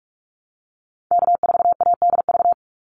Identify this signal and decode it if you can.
Morse code, heard as a keyed tone, about 44 wpm, 700 Hz. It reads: K4ADV
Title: NA SOTA Activator Callsigns 45wpm